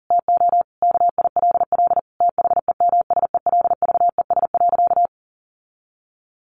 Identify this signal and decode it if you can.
Morse code, heard as a keyed tone, about 40 wpm, 700 Hz. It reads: TO KILL THEMSELVES.
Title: Sentences from Top 1400 Words 40wpm